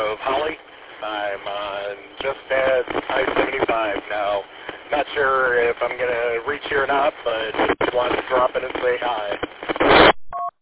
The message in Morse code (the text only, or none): none